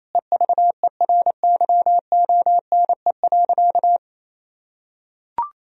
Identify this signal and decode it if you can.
Morse code, heard as a keyed tone, about 28 wpm, 700 Hz. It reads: EVERYONE.
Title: Sentences from Top 1000 Words - Review Part 3 - 28wpm